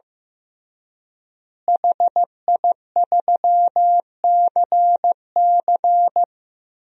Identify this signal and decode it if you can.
Morse code, heard as a keyed tone, about 16 wpm, 700 Hz. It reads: HI3CC
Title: DXCC-100 Call Signs - 5 Characters - Repeated 3x - 15wpm